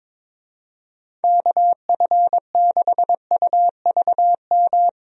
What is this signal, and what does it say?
Morse code, heard as a keyed tone, about 22 wpm, 700 Hz. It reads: KF6UVM
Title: USA Call Signs - 6 Characters - 22wpm